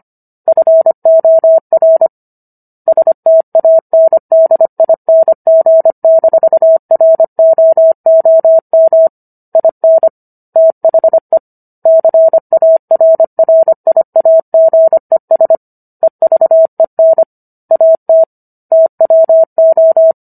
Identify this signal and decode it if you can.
Morse code, heard as a keyed tone, about 25 wpm, 650 Hz. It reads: FOR STANDING-ROOM IN THE CARRIAGES EVEN AT TWO
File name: Book One / 16-Chapter 16 - 25 WPM